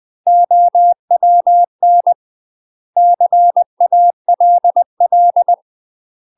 Morse code, heard as a keyed tone, about 20 wpm, 700 Hz. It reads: OWN CALL
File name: UK MASTER QSOs TRAINING / 20WPM / UK MASTER QSOs 0039